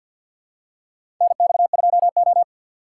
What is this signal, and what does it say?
Morse code, heard as a keyed tone, about 53 wpm, 700 Hz. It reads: NX2O
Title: USA Call Signs - 6x Character Spacing - 50wpm